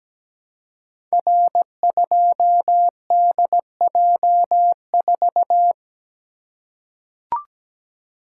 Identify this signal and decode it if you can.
Morse code, heard as a keyed tone, about 17 wpm, 700 Hz. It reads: R2DJ4
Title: Code Groups - 5 Letter:Numbers - 17wpm